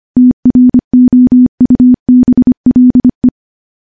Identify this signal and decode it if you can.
Morse code, heard as a keyed tone, about 25 wpm, 250 Hz. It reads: TROUBLE
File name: morse / morse-training-25WPM